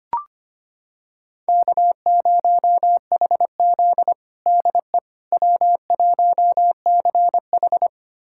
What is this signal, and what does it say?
Morse code, heard as a keyed tone, about 25 wpm, 700 Hz. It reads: K0HZ DE W1CH
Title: USA Callsign DE Callsign - 4 Characters - 25wpm